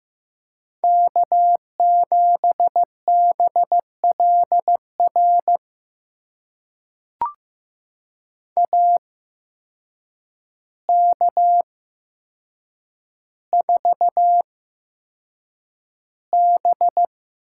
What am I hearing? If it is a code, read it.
Morse code, heard as a keyed tone, about 15 wpm, spaced as for 7 wpm, 700 Hz. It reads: K7BLR AK4B
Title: USA Call Signs - 5 Characters - 8x Character Spacing - 15wpm